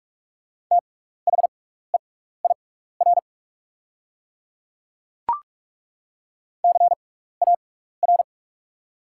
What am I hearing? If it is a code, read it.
Morse code, heard as a keyed tone, about 42 wpm, spaced as for 15 wpm, 700 Hz. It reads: THEIR CAR
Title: 6x Character Spacing - Top 500 Words - 45wpm